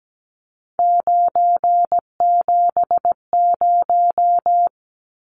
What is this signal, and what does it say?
Morse code, heard as a keyed tone, about 17 wpm, 700 Hz. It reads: 970
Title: Numbers - 3-Digit 17wpm